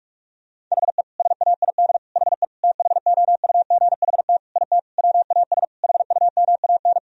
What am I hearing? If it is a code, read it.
Morse code, heard as a keyed tone, about 41 wpm, 700 Hz. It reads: HE SAID HE THOUGHT IT WAS HUMAN
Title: Sentences from Top 1000 Words - Review Part 3 - 45wpm